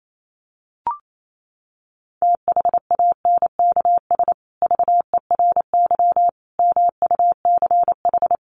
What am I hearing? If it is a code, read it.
Morse code, heard as a keyed tone, about 28 wpm, 700 Hz. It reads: THANKS VERY MUCH
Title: Sets of 5 Words Encore 28wpm